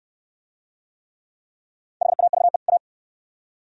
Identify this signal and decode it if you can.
Morse code, heard as a keyed tone, about 69 wpm, 700 Hz. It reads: HAVE A